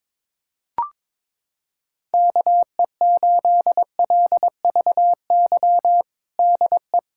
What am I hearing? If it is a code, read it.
Morse code, heard as a keyed tone, about 22 wpm, 700 Hz. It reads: KE8LVY DE
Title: USA Callsign DE Callsign - 6 Characters - 22wpm